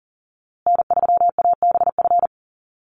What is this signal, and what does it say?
Morse code, heard as a keyed tone, about 40 wpm, 700 Hz. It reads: N3ABF
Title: USA Call Signs Repeated 2x 40wpm